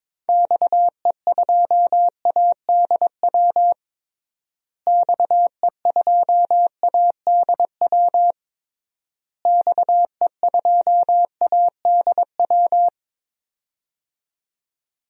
Morse code, heard as a keyed tone, about 22 wpm, 700 Hz. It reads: XE2ADW XE2ADW XE2ADW
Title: DXCC-100 Call Signs - 6 Characters - Repeated 3x - 22wpm